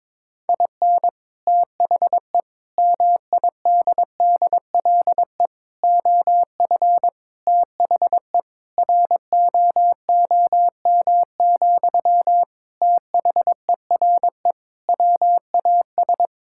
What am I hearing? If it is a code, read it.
Morse code, heard as a keyed tone, about 22 wpm, 700 Hz. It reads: IN THE MIDDLE OF THE ROOM, THERE WAS
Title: Sentences from Top 1000 Words - Review Part 6 - 22wpm